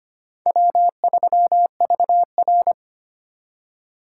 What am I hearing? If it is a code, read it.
Morse code, heard as a keyed tone, about 33 wpm, 700 Hz. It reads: W3VR